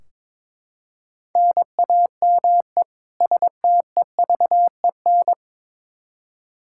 Morse code, heard as a keyed tone, about 22 wpm, 700 Hz. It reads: NAME STEVEN
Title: Practice QSO - Exchange 1 - Beginning - 22wpm